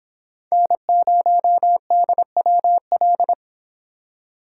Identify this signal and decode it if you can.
Morse code, heard as a keyed tone, about 25 wpm, 700 Hz. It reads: N0DWL